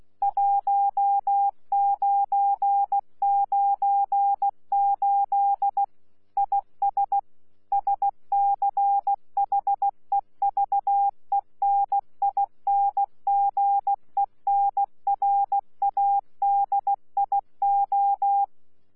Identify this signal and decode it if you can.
Morse code, heard as a keyed tone, about 16 wpm, 800 Hz. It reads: 1998 IS SCHEVENINGENRADIO
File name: Z PI4VRZ / PI4VRZ 14-jun-2025